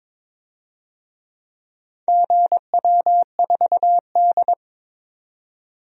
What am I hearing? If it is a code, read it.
Morse code, heard as a keyed tone, about 22 wpm, 700 Hz. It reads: GW4D
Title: DXCC-100 Call Signs - 4 Characters - 22wpm